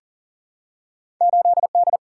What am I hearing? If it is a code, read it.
Morse code, heard as a keyed tone, about 40 wpm, 700 Hz. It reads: 8D
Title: Single-Letter Single-Number 40wpm